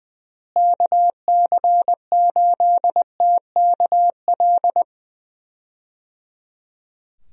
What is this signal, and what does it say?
Morse code, heard as a keyed tone, about 20 wpm, 700 Hz. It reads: KC8TKL
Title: USA Call Signs - 6 Characters - Repeated 3x - 20wpm